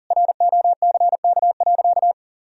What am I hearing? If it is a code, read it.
Morse code, heard as a keyed tone, about 40 wpm, 700 Hz. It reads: ROCK.
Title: Sentences from Top 1000 Words - Review Part 2 - 40wpm